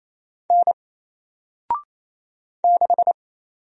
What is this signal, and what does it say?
Morse code, heard as a keyed tone, about 29 wpm, 700 Hz. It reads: N 6
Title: Single Letter-Number 28wpm